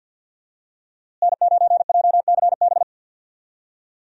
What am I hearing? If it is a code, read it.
Morse code, heard as a keyed tone, about 50 wpm, 700 Hz. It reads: N9JCB
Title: USA Call Signs - 5 Characters - Repeated 3x - 50wpm